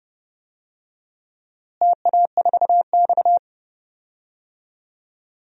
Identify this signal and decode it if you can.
Morse code, heard as a keyed tone, about 29 wpm, 700 Hz. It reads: TA4X
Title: DXCC-100 Call Signs - 4 Characters - Repeated 3x - 30wpm